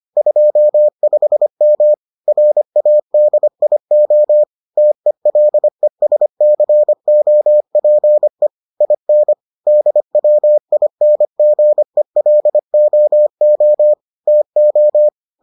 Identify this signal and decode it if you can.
Morse code, heard as a keyed tone, about 25 wpm, 600 Hz. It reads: 25M RADIO TELESCOPE IN DWINGELOO TO